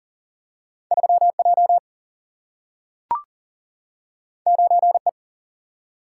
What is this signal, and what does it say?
Morse code, heard as a keyed tone, about 40 wpm, 700 Hz. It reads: 3J 9E
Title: Single-Letter Single-Number 40wpm